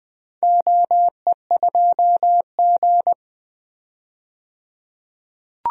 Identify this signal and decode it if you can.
Morse code, heard as a keyed tone, about 21 wpm, 700 Hz. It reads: OE2G
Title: DXCC-100 Call Signs - 4 Characters - 20wpm